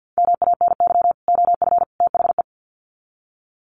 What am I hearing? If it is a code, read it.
Morse code, heard as a keyed tone, about 50 wpm, 700 Hz. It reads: MANY OF THE
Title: Sets of 3 Words 50wpm